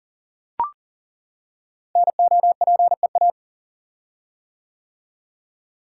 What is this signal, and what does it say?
Morse code, heard as a keyed tone, about 40 wpm, 700 Hz. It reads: NOPEA
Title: Code Groups - 5 Letters - 40wpm